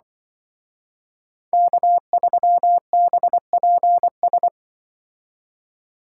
Morse code, heard as a keyed tone, about 24 wpm, 700 Hz. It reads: K3BPS
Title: Speed Racing - USA 1x3 Call Signs 20wpm